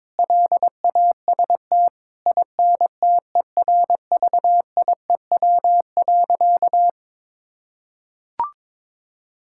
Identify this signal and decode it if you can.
Morse code, heard as a keyed tone, about 22 wpm, 700 Hz. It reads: LAST INTERVIEW.
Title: Sentences from Top 1600 Words 22wpm